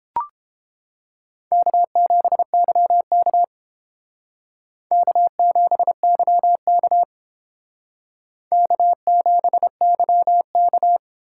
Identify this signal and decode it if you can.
Morse code, heard as a keyed tone, about 29 wpm, 700 Hz. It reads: K7YK K7YK K7YK